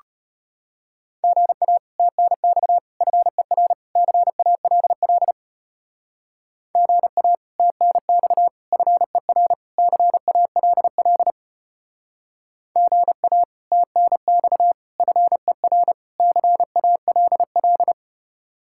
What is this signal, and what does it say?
Morse code, heard as a keyed tone, about 33 wpm, 700 Hz. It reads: GA TNX FER CALL GA TNX FER CALL GA TNX FER CALL